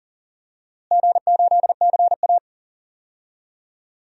Morse code, heard as a keyed tone, about 40 wpm, 700 Hz. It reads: G8CA
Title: DXCC-100 Call Signs - 4 Characters - Repeated 2x - 40wpm